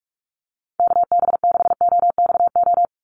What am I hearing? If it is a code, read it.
Morse code, heard as a keyed tone, about 44 wpm, 700 Hz. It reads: KB6OXO
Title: USA Call Signs 45wpm